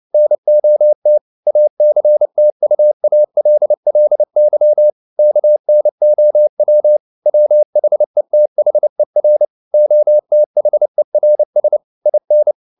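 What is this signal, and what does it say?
Morse code, heard as a keyed tone, about 29 wpm, 600 Hz. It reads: NOT ACTUALLY KNOW WHETHER OTHERS IN
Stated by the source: Morse code at 29 wpm, sending Ken Follett's novel Never in English, from CW med Gnister.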